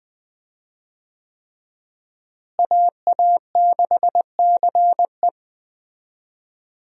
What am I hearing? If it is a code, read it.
Morse code, heard as a keyed tone, about 20 wpm, 700 Hz. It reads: AA6CE
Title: USA Call Signs - 5 Characters - Repeated 3x - 20wpm